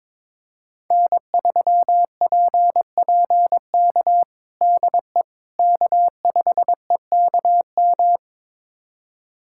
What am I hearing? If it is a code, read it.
Morse code, heard as a keyed tone, about 22 wpm, 700 Hz. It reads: N3PPK DE K5EKM
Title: USA Callsign DE Callsign - 5 Characters - 22wpm